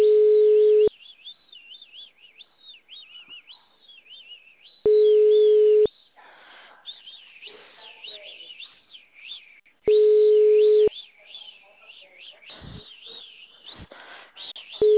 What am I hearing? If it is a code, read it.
Ringback tone (Europe, 425 Hz).